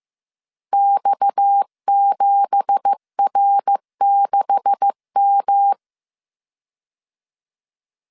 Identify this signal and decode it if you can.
Morse code, heard as a keyed tone, about 15 wpm, 800 Hz. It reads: X7R6M
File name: CW / DET / 1 9 1500C S A 14wpm 3222